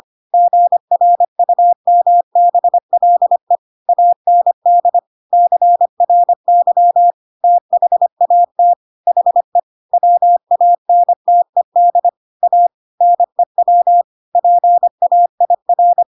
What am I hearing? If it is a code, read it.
Morse code, heard as a keyed tone, about 25 wpm, 700 Hz. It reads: GRUMBLE AND CRY THAT HE WANTED A NEW PAIR
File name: Pinocchio / Pinocchio ch8 25 25 ik1hge